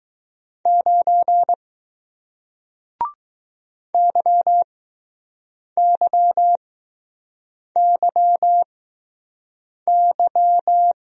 Morse code, heard as a keyed tone, about 21 wpm, spaced as for 6 wpm, 700 Hz. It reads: 9 YYYY